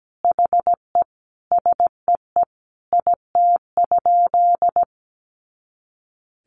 Morse code, heard as a keyed tone, about 17 wpm, 700 Hz. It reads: HE SEE IT?
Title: Sentences from Top 100 Words 17wpm